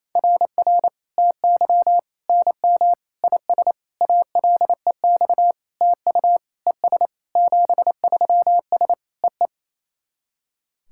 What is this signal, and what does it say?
Morse code, heard as a keyed tone, about 28 wpm, 690 Hz. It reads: RR TY NM IS ALEX TU ES 73S EE
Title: SOTA QSO Practice 28wpm